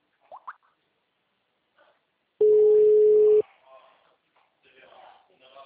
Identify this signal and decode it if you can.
Ringback tone (Europe, 425 Hz), with speech.